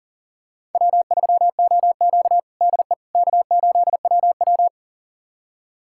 Morse code, heard as a keyed tone, about 40 wpm, 700 Hz. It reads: W3OQ DE K8WW
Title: USA Callsign DE Callsign - 4 Characters - 40wpm